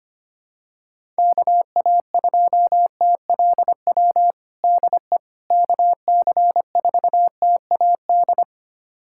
Morse code, heard as a keyed tone, about 25 wpm, 700 Hz. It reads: KA2TLW DE KC4TAD
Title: USA Callsign DE Callsign - 6 Characters - 25wpm